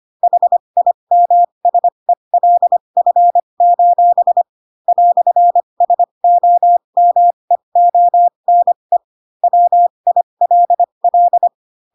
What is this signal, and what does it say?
Morse code, heard as a keyed tone, about 25 wpm, spaced as for 22 wpm, 700 Hz. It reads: HIMSELF: "SOMEONE WILL
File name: Pinocchio / Pinocchio ch6 18 25 ik1hge